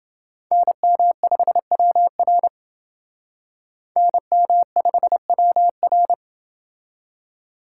Morse code, heard as a keyed tone, about 28 wpm, 700 Hz. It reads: NM5WR NM5WR